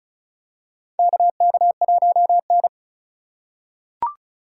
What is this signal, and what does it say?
Morse code, heard as a keyed tone, about 34 wpm, 700 Hz. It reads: KK1N
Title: USA Call Signs - 2x Character Spacing - 35wpm